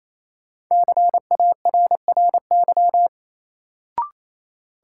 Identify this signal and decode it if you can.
Morse code, heard as a keyed tone, about 28 wpm, 700 Hz. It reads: CARRY
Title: Sentences from Top 400 Words - New Word Drill - 28wpm